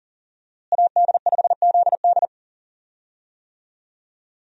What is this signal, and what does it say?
Morse code, heard as a keyed tone, about 37 wpm, 700 Hz. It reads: AD5ZD
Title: USA Call Signs - 5 Characters - Repeated 3x - 40wpm